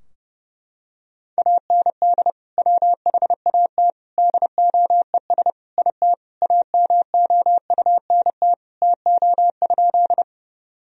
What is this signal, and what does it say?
Morse code, heard as a keyed tone, about 32 wpm, 700 Hz. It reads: AND WHAT DOES IT AMOUNT TO?
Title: Sentences from Top 1000 Words - Review Part 4 - 30wpm